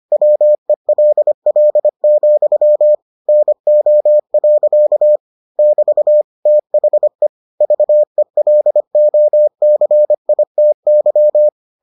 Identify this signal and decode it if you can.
Morse code, heard as a keyed tone, about 25 wpm, 600 Hz. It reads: WELL, NO. <BT> THE VELOCITY